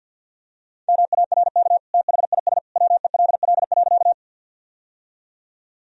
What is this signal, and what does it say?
Morse code, heard as a keyed tone, about 53 wpm, 700 Hz. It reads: MARK THIS WELL.
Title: Sentences from Top 1000 Words - Review Part 1 - 50wpm